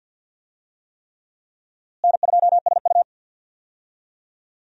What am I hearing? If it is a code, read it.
Morse code, heard as a keyed tone, about 50 wpm, 700 Hz. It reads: N2SU
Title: USA Call Signs - 4 Characters - Repeated 3x - 50wpm